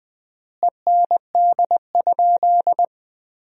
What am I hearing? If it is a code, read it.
Morse code, heard as a keyed tone, about 20 wpm, 700 Hz. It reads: END?
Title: 3x Word-Spacing - Sentences from Top 200 Words - 20wpm